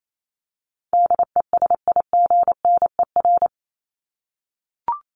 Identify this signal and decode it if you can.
Morse code, heard as a keyed tone, about 28 wpm, 700 Hz. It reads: DESIGNER
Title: Top 1501-1750 Words 28wpm